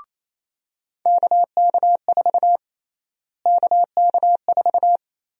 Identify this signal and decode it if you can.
Morse code, heard as a keyed tone, about 28 wpm, 700 Hz. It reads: KK4 KK4
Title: USA Call Sign Prefixes - Repeated 2x - 28wpm